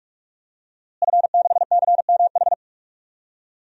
Morse code, heard as a keyed tone, about 41 wpm, 700 Hz. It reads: F6CMH
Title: DXCC-100 Call Signs - 5 Characters - Repeated 3x - 45wpm